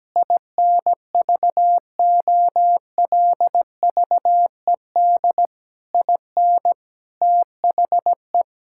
Morse code, heard as a keyed tone, about 17 wpm, 700 Hz. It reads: INVOLVED IN THE
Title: Sentences from Top 1000 Words - Review Part 5 - 17wpm